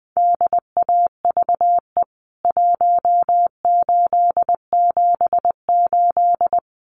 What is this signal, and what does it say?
Morse code, heard as a keyed tone, about 20 wpm, 700 Hz. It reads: DAVE 1878
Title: CWT Exchange 20wpm